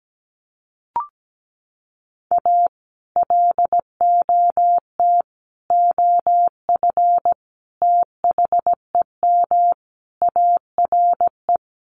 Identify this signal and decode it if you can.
Morse code, heard as a keyed tone, about 17 wpm, 700 Hz. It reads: A LOT OF THEM ARE
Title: Sets of 5 Words Encore2 17wpm